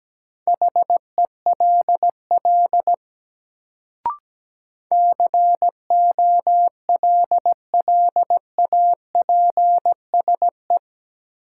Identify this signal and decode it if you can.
Morse code, heard as a keyed tone, about 17 wpm, 700 Hz. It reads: HELL COLLAPSE
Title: Sets of 5 Words Encore - New Word Drill - 17wpm